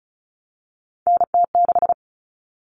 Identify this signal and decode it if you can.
Morse code, heard as a keyed tone, about 34 wpm, 700 Hz. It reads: NT6
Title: All Cut Numbers - 3-Digit 35wpm